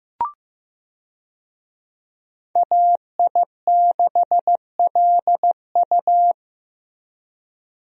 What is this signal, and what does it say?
Morse code, heard as a keyed tone, about 16 wpm, 700 Hz. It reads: AI6LU
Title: USA Call Signs - 5 Characters - Repeated 3x - 15wpm